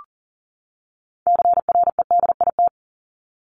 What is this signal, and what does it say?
Morse code, heard as a keyed tone, about 40 wpm, 700 Hz. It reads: CREDIT
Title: Top 751-1000 Words 40wpm